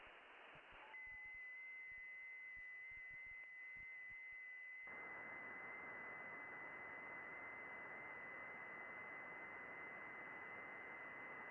FT8.
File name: contests / june-vhf-k5tr- -0200